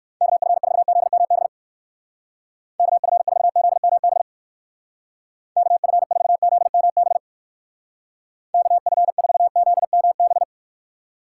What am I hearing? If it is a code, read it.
Morse code, heard as a keyed tone, about 56 wpm, 700 Hz. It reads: KF4ZMB KF4ZMB KF4ZMB KF4ZMB